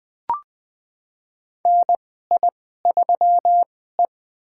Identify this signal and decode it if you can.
Morse code, heard as a keyed tone, about 20 wpm, spaced as for 14 wpm, 700 Hz. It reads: NI3E
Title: USA Call Signs - 4 Characters - 2x Character Spacing - 20wpm